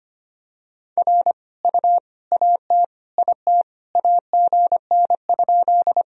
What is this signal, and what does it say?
Morse code, heard as a keyed tone, about 25 wpm, 700 Hz. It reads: R U AT IT AGN?
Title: Sentences from Top 200 Words and Abbreviations 25wpm